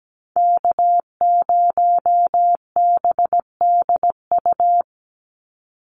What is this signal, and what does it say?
Morse code, heard as a keyed tone, about 17 wpm, 700 Hz. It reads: K0BDU
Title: USA Call Signs 17wpm